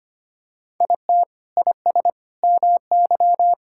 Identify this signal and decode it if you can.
Morse code, heard as a keyed tone, about 25 wpm, 700 Hz. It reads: IT IS MY
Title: Sentences from Top 1600 Words 25wpm